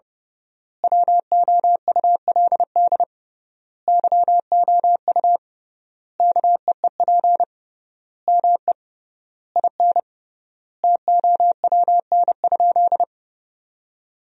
Morse code, heard as a keyed tone, about 30 wpm, 700 Hz. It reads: WOULD YOU KEEP ME IN TOWN?
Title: 3x Word-Spacing - Sentences from Top 500 Words - 30wpm